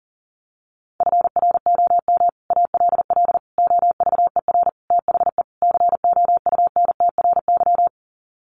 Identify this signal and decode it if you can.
Morse code, heard as a keyed tone, about 40 wpm, 700 Hz. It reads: FROM ALL OVER THE COUNTRY
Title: Sets of 5 Words Encore 40wpm